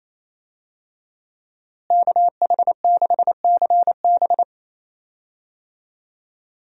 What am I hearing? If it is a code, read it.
Morse code, heard as a keyed tone, about 28 wpm, 700 Hz. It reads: KH6CB
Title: USA Call Signs - 5 Characters - Repeated 2x - 28wpm